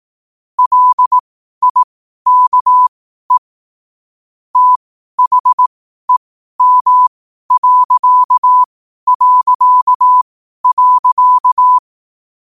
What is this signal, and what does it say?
Morse code, heard as a keyed tone, about 18 wpm, spaced as for 12 wpm, 1000 Hz. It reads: LIKE THEM...
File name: qotdfiles / 12wpm